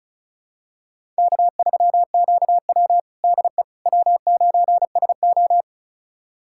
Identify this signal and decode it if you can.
Morse code, heard as a keyed tone, about 36 wpm, 700 Hz. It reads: K3QW DE W9SO
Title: USA Callsign DE Callsign - 4 Characters - 35wpm